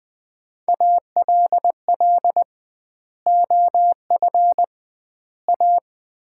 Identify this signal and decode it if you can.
Morse code, heard as a keyed tone, about 21 wpm, 700 Hz. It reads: ALL OF A
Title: 2x Word-Spacing - Sets of 4 Words - 20wpm